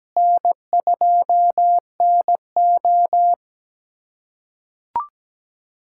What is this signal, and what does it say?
Morse code, heard as a keyed tone, about 17 wpm, 700 Hz. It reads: N2NO
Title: USA Callsign DE Callsign 17wpm